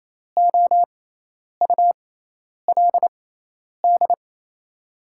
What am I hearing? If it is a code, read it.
Morse code, heard as a keyed tone, about 28 wpm, spaced as for 10 wpm, 700 Hz. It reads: OULD
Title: 6x Character Spacing - Top 500 Words - 28wpm